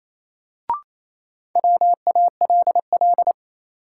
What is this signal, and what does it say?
Morse code, heard as a keyed tone, about 28 wpm, 700 Hz. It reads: WALL
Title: Sentences from Top 1100 Words - New Word Drill Alt - 28wpm